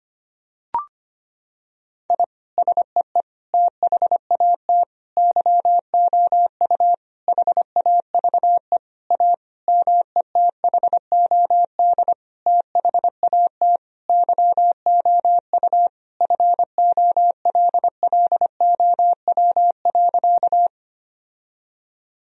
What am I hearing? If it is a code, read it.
Morse code, heard as a keyed tone, about 25 wpm, 700 Hz. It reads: I SEE THAT YOU HAVE A METHOD THAT YOU FOLLOW.
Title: Sentences from Top 1000 Words - Review Part 3 - 25wpm